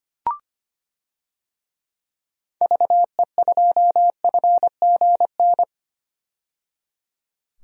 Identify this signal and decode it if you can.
Morse code, heard as a keyed tone, about 25 wpm, 690 Hz. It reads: VE2FGN
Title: DXCC-100 Call Signs - 6 Characters - 25wpm